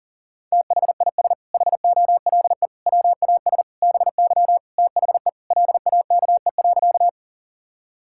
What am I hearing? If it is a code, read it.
Morse code, heard as a keyed tone, about 37 wpm, 700 Hz. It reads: THIS HOLE WAS BY THE LAKE.
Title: Sentences from Top 1600 Words 40wpm